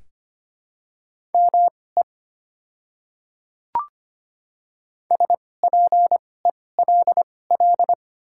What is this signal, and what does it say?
Morse code, heard as a keyed tone, about 25 wpm, spaced as for 18 wpm, 690 Hz. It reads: ME SPELL
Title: 2x Character Spacing - Top 500 Words - 25wpm